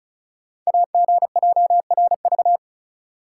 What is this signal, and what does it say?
Morse code, heard as a keyed tone, about 36 wpm, 700 Hz. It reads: AGJRV
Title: Code Groups - 5 Letter:Numbers - 35wpm